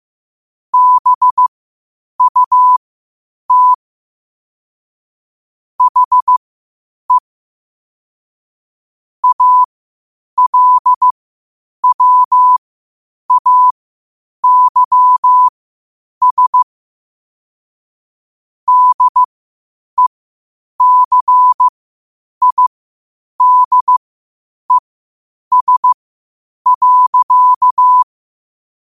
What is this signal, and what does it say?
Morse code, heard as a keyed tone, about 15 wpm, spaced as for 8 wpm, 1000 Hz. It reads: BUT HE ALWAYS DECIDES.
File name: qotdfiles / 8wpm